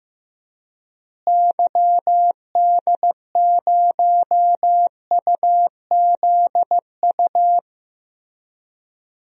Morse code, heard as a keyed tone, about 16 wpm, 700 Hz. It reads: YD0UZU
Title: DXCC-100 Call Signs - 6 Characters - Repeated 3x - 15wpm